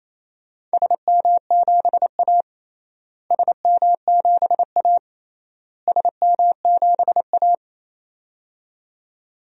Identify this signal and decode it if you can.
Morse code, heard as a keyed tone, about 28 wpm, 700 Hz. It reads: SM7A SM7A SM7A
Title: DXCC-100 Call Signs - 4 Characters - Repeated 3x - 28wpm